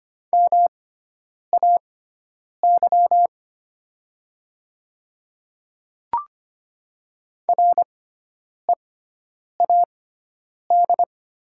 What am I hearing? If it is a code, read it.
Morse code, heard as a keyed tone, about 25 wpm, spaced as for 9 wpm, 700 Hz. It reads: MAY READ
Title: 6x Character Spacing - Top 300 Words - 25wpm